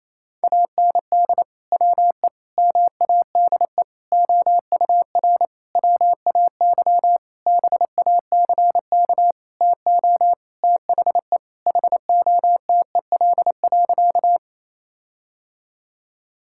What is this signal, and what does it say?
Morse code, heard as a keyed tone, about 28 wpm, 700 Hz. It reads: AND WE MADE OUR WAY BACK TO THE HOTEL.
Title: Sentences from Top 1600 Words 28wpm